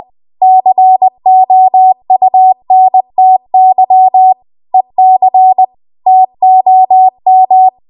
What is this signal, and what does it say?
Morse code, heard as a keyed tone, about 20 wpm, 750 Hz. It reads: COUNTY EC TOM